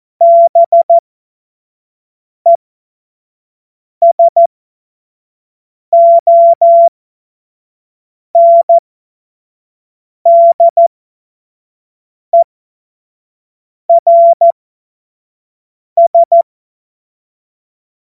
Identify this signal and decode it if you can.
Morse code, heard as a keyed tone, about 14 wpm, spaced as for 5 wpm, 680 Hz. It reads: BESONDERS
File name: Morsekurs / Witze / Witz 20. Januar 2026 25 F70